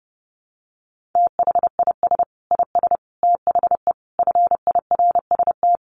Morse code, heard as a keyed tone, about 30 wpm, 700 Hz. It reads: THIS IS THE FIRST
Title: Sets of 5 Words 30wpm